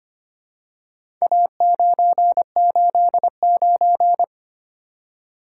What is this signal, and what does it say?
Morse code, heard as a keyed tone, about 25 wpm, 700 Hz. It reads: A989
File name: practice / 25wpm / Four Letter-Number Combinations - 25wpm